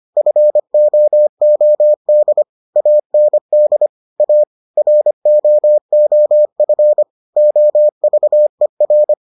Morse code, heard as a keyed tone, about 25 wpm, 600 Hz. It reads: FOOD AND A ROOF OVER